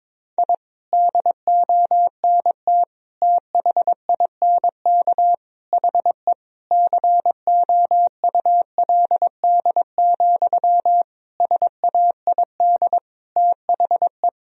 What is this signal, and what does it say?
Morse code, heard as a keyed tone, about 22 wpm, 700 Hz. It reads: I DONT THINK HE COULD, SAID THE
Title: Sentences from Top 1000 Words - Review Part 4 - 22wpm